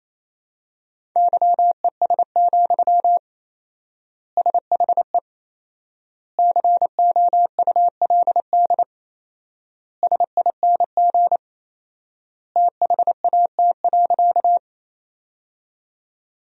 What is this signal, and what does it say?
Morse code, heard as a keyed tone, about 28 wpm, 700 Hz. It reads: YES, SHE COULD SING THAT.
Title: 4x Word-Spacing - Sentences from Top 500 Words - 28wpm